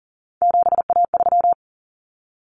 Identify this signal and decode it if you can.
Morse code, heard as a keyed tone, about 40 wpm, 700 Hz. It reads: 7A3
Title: Common Cut Numbers - 3-Digit 40wpm